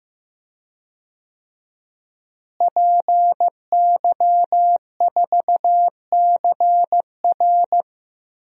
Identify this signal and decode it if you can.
Morse code, heard as a keyed tone, about 15 wpm, 700 Hz. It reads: PY4CR
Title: DXCC-100 Call Signs - 5 Characters - Repeated 2x - 15wpm